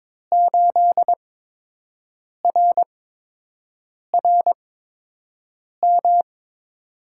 Morse code, heard as a keyed tone, about 22 wpm, spaced as for 6 wpm, 700 Hz. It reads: 8RRM
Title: USA Call Signs - 6 Characters - 8x Character Spacing - 22wpm